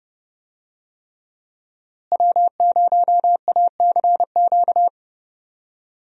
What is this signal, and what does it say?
Morse code, heard as a keyed tone, about 30 wpm, 700 Hz. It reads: W0ACQ
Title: USA Call Signs - 5 Characters - Repeated 3x - 30wpm